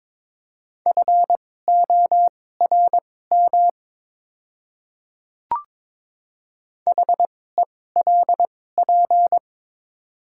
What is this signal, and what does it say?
Morse code, heard as a keyed tone, about 22 wpm, spaced as for 16 wpm, 690 Hz. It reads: FORM HELP
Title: 2x Character Spacing - Top 500 Words - 22wpm